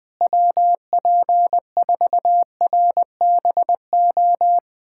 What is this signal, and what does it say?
Morse code, heard as a keyed tone, about 21 wpm, 700 Hz. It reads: WP4RBO
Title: USA Callsign DE Callsign 20wpm